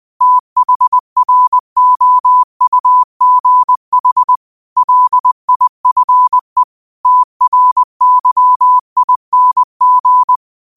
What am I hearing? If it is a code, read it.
Morse code, heard as a keyed tone, about 20 wpm, 1000 Hz. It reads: THROUGH LIFE TRYING
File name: qotdfiles / 20wpm